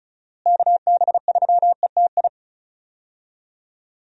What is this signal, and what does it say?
Morse code, heard as a keyed tone, about 36 wpm, 700 Hz. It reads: KB3ETI
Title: USA Call Signs - 6 Characters - Repeated 3x - 35wpm